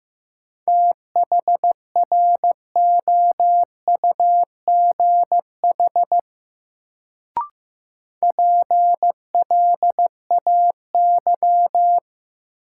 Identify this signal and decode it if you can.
Morse code, heard as a keyed tone, about 15 wpm, 700 Hz. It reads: THROUGH PLAY
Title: Sentences from Top 200 Words - New Word Drill - 15wpm